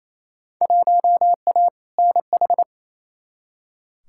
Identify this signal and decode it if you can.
Morse code, heard as a keyed tone, about 28 wpm, 700 Hz. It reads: 1A NH
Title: Field Day Exchanges Encore - 28wpm